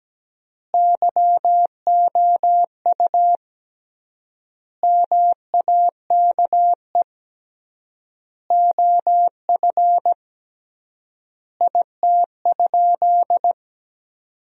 Morse code, heard as a keyed tone, about 17 wpm, 700 Hz. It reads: YOU MAKE OF IT?
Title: 3x Word-Spacing - Sentences from Top 100 Words - 17wpm